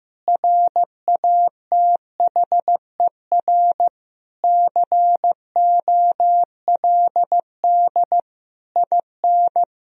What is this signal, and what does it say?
Morse code, heard as a keyed tone, about 15 wpm, 700 Hz. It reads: RATHER COLD IN
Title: Sentences from Top 1000 Words - Review Part 4 - 15wpm